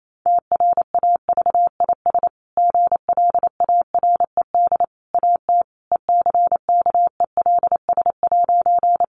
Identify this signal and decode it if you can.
Morse code, heard as a keyed tone, about 28 wpm, 700 Hz. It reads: TRAVIS GLARED AT ECKELS'
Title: A Sound Of Thunder 28wpm